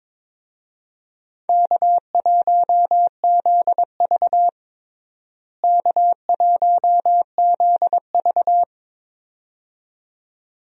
Morse code, heard as a keyed tone, about 22 wpm, 700 Hz. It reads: K1ZV K1ZV
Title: USA Call Signs - 4 Characters - Repeated 2x - 22wpm